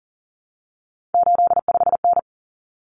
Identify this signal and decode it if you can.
Morse code, heard as a keyed tone, about 40 wpm, 700 Hz. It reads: 85N
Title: Common Cut Numbers - 3-Digit 40wpm